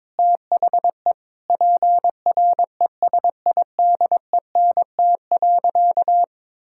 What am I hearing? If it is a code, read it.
Morse code, heard as a keyed tone, about 22 wpm, 700 Hz. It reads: THE PRESIDENT.
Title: Sentences from Top 1600 Words 22wpm